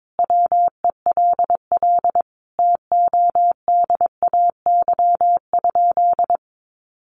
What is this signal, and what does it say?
Morse code, heard as a keyed tone, about 22 wpm, 700 Hz. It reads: WELL TODAY?
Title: Sentences from Top 1200 Words 22wpm